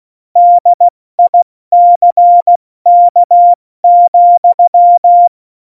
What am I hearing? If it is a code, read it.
Morse code, heard as a keyed tone, about 16 wpm, spaced as for 14 wpm, 700 Hz. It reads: DICK,
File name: Morsekurs / Witze / Witz 23. April 2019, 70 F80